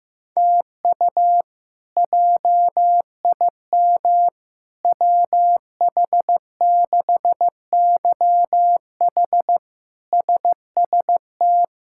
Morse code, heard as a keyed tone, about 16 wpm, 700 Hz. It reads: TU JIM WH6YH SST
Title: K1USN SST QSO Practice - 15wpm